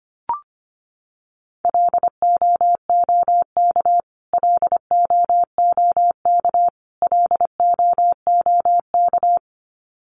Morse code, heard as a keyed tone, about 25 wpm, 700 Hz. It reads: LOOK LOOK LOOK
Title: Top 200 Words Repeated 25wpm